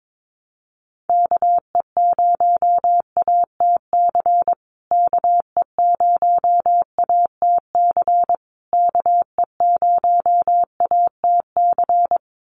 Morse code, heard as a keyed tone, about 22 wpm, 700 Hz. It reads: KE0ATC KE0ATC KE0ATC
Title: USA Call Signs Repeated 3x 22wpm